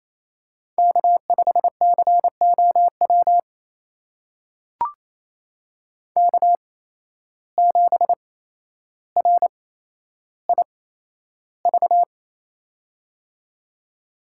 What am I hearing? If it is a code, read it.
Morse code, heard as a keyed tone, about 28 wpm, spaced as for 12 wpm, 700 Hz. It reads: K5COW K7RIV
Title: USA Call Signs - 5 Characters - 8x Character Spacing - 28wpm